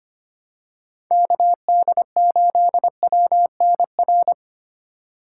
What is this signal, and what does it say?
Morse code, heard as a keyed tone, about 25 wpm, 700 Hz. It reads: KD8WNR
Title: USA Call Signs - 6x Character Spacing - 25wpm